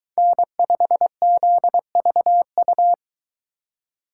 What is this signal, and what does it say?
Morse code, heard as a keyed tone, about 22 wpm, 700 Hz. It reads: N5ZVU